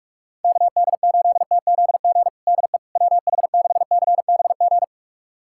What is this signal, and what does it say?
Morse code, heard as a keyed tone, about 41 wpm, 700 Hz. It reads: KD8TZG DE WH6CBG
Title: USA Callsign DE Callsign - 6 Characters - 45wpm